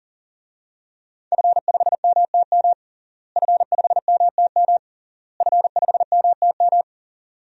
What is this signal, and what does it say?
Morse code, heard as a keyed tone, about 40 wpm, 700 Hz. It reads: F5MTM F5MTM F5MTM
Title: DXCC-100 Call Signs - 5 Characters - Repeated 3x - 40wpm